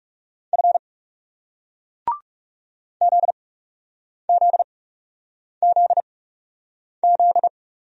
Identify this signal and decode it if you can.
Morse code, heard as a keyed tone, about 43 wpm, spaced as for 8 wpm, 700 Hz. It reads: F ZZZZ